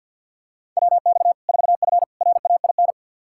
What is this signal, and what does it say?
Morse code, heard as a keyed tone, about 52 wpm, 700 Hz. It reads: WX 4F RAIN
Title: Practice QSO - Exchange 2 - Info - 50wpm